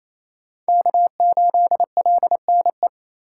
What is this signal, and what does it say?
Morse code, heard as a keyed tone, about 28 wpm, 700 Hz. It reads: K8LNE
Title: USA Call Signs - 3x Character Spacing - 28wpm